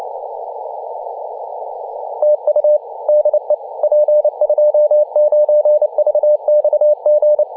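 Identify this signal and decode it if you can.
Morse code, heard as a keyed tone, about 28 wpm, 600 Hz. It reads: TU DE P29VXG
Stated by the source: DX station in Papua New Guinea on 21.040 MHz in CW.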